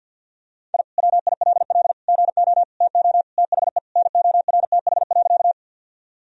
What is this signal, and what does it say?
Morse code, heard as a keyed tone, about 47 wpm, 700 Hz. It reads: I WILL GO TO THE NORTH.
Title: Sentences from Top 1000 Words - Review Part 1 - 50wpm